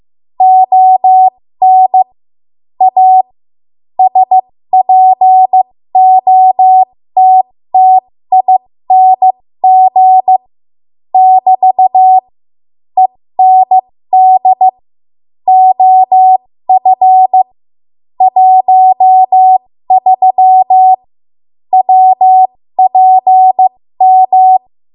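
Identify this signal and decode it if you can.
Morse code, heard as a keyed tone, about 15 wpm, spaced as for 13 wpm, 750 Hz. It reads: ON A SPOTTING <BT> END OF 13 WPM
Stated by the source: 13 wpm Morse code practice archive transmission.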